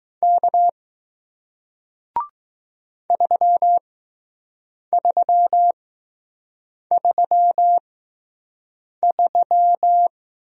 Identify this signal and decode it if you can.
Morse code, heard as a keyed tone, about 20 wpm, spaced as for 6 wpm, 700 Hz. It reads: K 3333